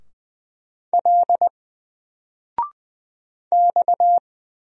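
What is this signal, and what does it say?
Morse code, heard as a keyed tone, about 20 wpm, 700 Hz. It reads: L X
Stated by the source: Single letter and number Morse drill at 20 wpm, from N4LCD.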